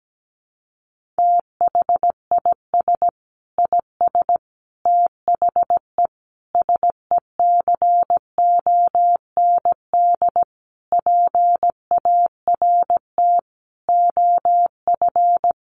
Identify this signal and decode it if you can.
Morse code, heard as a keyed tone, about 17 wpm, 700 Hz. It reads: THIS IS THE SECOND PART OF
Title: Sentences from Top 1400 Words 17wpm